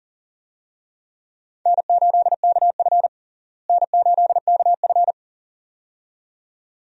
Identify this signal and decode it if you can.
Morse code, heard as a keyed tone, about 40 wpm, 700 Hz. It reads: N8KF N8KF
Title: USA Call Signs - 4 Characters - Repeated 2x - 40wpm